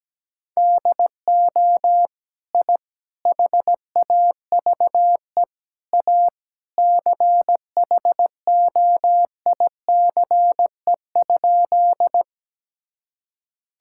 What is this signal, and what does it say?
Morse code, heard as a keyed tone, about 17 wpm, 700 Hz. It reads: DO I HAVE A CHOICE?
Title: Sentences from Top 1000 Words - Review Part 6 - 17wpm